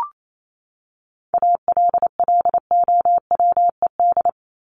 Morse code, heard as a keyed tone, about 28 wpm, 700 Hz. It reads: ALLOWED
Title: ED Suffix - Top 100 Words with Base Words 28wpm